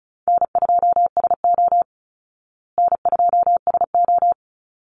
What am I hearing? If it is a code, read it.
Morse code, heard as a keyed tone, about 34 wpm, 700 Hz. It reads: N2SO N2SO
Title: Easy Call Signs Repeated 2x - 35wpm